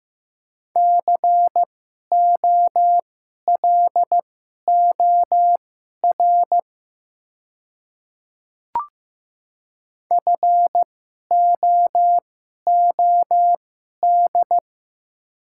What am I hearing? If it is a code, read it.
Morse code, heard as a keyed tone, about 15 wpm, spaced as for 11 wpm, 700 Hz. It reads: COLOR FOOD
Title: 2x Character Spacing - Top 500 Words - 15wpm